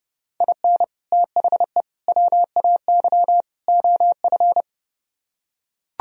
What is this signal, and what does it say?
Morse code, heard as a keyed tone, about 30 wpm, 700 Hz. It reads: IN THE WAY OF